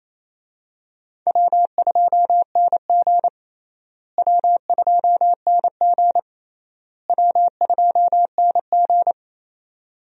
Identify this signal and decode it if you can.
Morse code, heard as a keyed tone, about 28 wpm, 700 Hz. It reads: W2NG W2NG W2NG
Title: USA Call Signs - 4 Characters - Repeated 3x - 28wpm